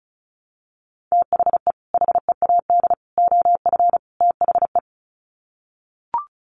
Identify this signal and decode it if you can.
Morse code, heard as a keyed tone, about 34 wpm, 700 Hz. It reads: THE HEAD OF THE
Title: Sets of 4 Words 35wpm